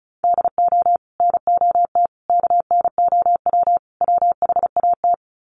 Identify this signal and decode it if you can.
Morse code, heard as a keyed tone, about 34 wpm, 700 Hz. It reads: DO NOT KNOW WHAT
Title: Sets of 4 Words 35wpm